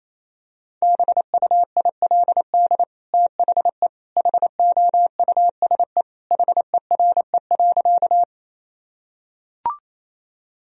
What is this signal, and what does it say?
Morse code, heard as a keyed tone, about 28 wpm, 700 Hz. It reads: BUILD THE HOUSE HERE.
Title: Sentences from Top 1000 Words - Review Part 1 - 28wpm